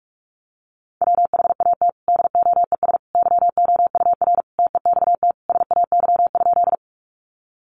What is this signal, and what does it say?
Morse code, heard as a keyed tone, about 44 wpm, 700 Hz. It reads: WHAT DOES YOUR TEXT SAY?
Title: Sentences from Top 1200 Words 45wpm